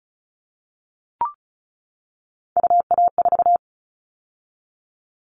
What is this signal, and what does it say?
Morse code, heard as a keyed tone, about 36 wpm, 700 Hz. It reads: UA4
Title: Callsign Prefixes for Top 100 DXCC - 35wpm